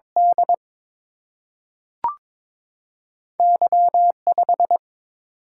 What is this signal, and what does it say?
Morse code, heard as a keyed tone, about 22 wpm, 700 Hz. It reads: D Y5
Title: Single-Letter Single-Number 22wpm